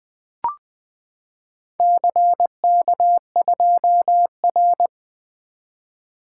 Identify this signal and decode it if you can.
Morse code, heard as a keyed tone, about 20 wpm, 700 Hz. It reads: CK2R
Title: Four Letter-Number Combinations - 20wpm